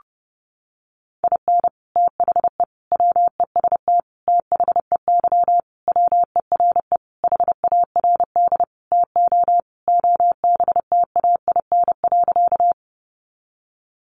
Morse code, heard as a keyed tone, about 30 wpm, 700 Hz. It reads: IN THE WEST THEY WERE HARD TO OBTAIN.
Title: Sentences from Top 1300 Words 30wpm